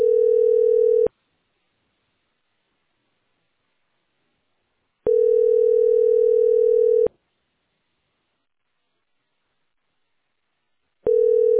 Ringback tone (North America), 440 + 480 Hz.